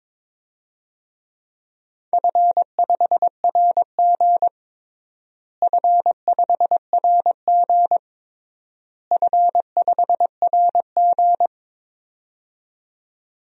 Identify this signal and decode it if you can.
Morse code, heard as a keyed tone, about 22 wpm, 700 Hz. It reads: F5RG F5RG F5RG
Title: DXCC-100 Call Signs - 4 Characters - Repeated 3x - 22wpm